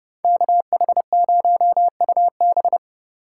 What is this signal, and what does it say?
Morse code, heard as a keyed tone, about 34 wpm, 700 Hz. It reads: KH0UB